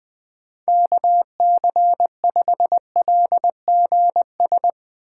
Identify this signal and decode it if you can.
Morse code, heard as a keyed tone, about 21 wpm, 700 Hz. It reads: KC5LGS